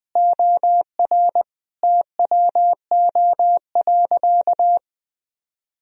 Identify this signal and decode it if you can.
Morse code, heard as a keyed tone, about 21 wpm, 700 Hz. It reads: OR TWO.
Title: Sentences from Top 1000 Words - Review Part 2 - 20wpm